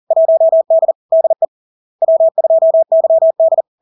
Morse code, heard as a keyed tone, about 40 wpm, 650 Hz. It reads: 1D DE W2YD
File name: LARK-FD-2022-Log-in-CW-40wpm-650-Hz